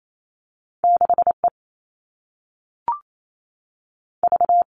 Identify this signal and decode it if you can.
Morse code, heard as a keyed tone, about 28 wpm, 700 Hz. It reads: 6E V
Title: All Cut Numbers - 2-Digit 28wpm